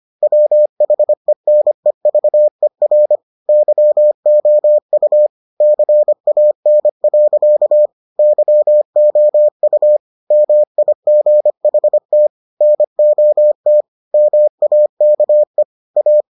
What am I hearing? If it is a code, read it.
Morse code, heard as a keyed tone, about 25 wpm, 600 Hz. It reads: WHENEVER YOU CAN. YOU MIGHT NOT MAKE A